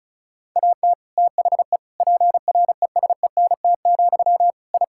Morse code, heard as a keyed tone, about 36 wpm, 700 Hz. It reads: AT THE PRESENT, I
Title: Sentences from Top 1000 Words - Review Part 5 - 35wpm